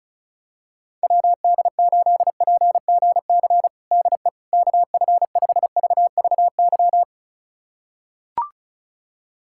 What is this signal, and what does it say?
Morse code, heard as a keyed tone, about 34 wpm, 700 Hz. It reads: WD8PGC DE KF5VVY
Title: USA Callsign DE Callsign - 6 Characters - 35wpm